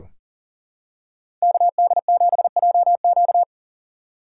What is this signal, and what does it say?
Morse code, heard as a keyed tone, about 40 wpm, 700 Hz. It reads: KD7JQ
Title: USA Call Signs - 5 Characters - 4x Character Spacing - 40wpm